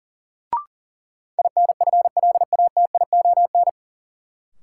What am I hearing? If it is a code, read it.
Morse code, heard as a keyed tone, about 40 wpm, 700 Hz. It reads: INFLATION
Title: Sentences from Top 1100 Words - New Word Drill - 40wpm